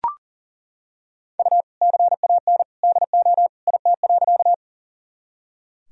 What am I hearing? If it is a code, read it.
Morse code, heard as a keyed tone, about 37 wpm, 700 Hz. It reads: U CAN DO IT.
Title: Sentences from Top 100 Words and Abbreviations 40wpm